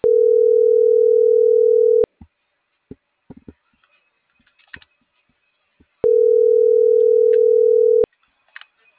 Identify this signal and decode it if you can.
Ringback tone (North America), 440 + 480 Hz.